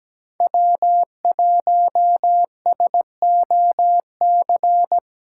Morse code, heard as a keyed tone, about 17 wpm, 700 Hz. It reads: W1SOC
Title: USA Callsign DE Callsign - 5 Characters - 17wpm